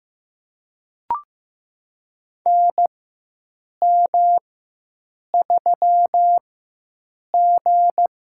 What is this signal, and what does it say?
Morse code, heard as a keyed tone, about 15 wpm, spaced as for 7 wpm, 700 Hz. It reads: NM3G
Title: USA Call Signs - 4 Characters - 4x Character Spacing - 15wpm